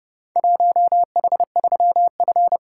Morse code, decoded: 1H3F

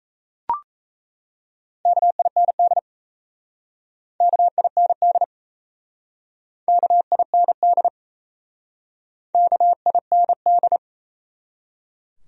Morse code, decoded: KIND KIND KIND KIND

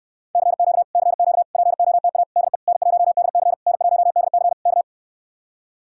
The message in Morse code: CQ CQ POTA DE N1DC N1DC K

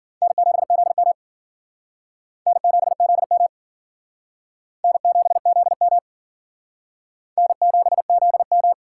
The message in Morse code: N7ZM N7ZM N7ZM N7ZM

